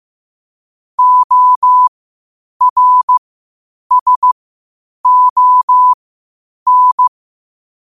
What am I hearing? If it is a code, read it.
Morse code, heard as a keyed tone, about 15 wpm, spaced as for 8 wpm, 1000 Hz. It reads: ORSON
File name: 8wpm